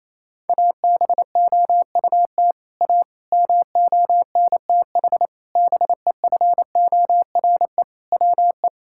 Morse code, heard as a keyed tone, about 28 wpm, 700 Hz. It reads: ABOUT A MONTH BEFORE WE
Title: Sentences from Top 1000 Words - Review Part 5 - 28wpm